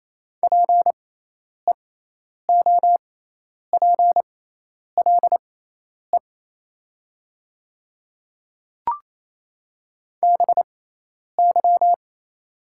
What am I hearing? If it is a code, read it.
Morse code, heard as a keyed tone, about 28 wpm, spaced as for 10 wpm, 690 Hz. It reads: PEOPLE BY
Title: 6x Character Spacing - Top 100 Words - 28wpm